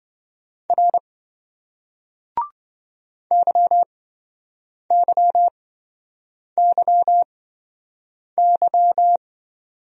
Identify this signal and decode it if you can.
Morse code, heard as a keyed tone, about 27 wpm, spaced as for 7 wpm, 700 Hz. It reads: R YYYY